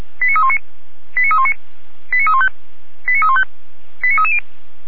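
Five-tone selective call (ZVEI), heard as quick runs of five tones, tones of about 70 ms. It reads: 89319 89319 89315 89315 89330
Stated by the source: ZVEI five-tone selective call received by SDR.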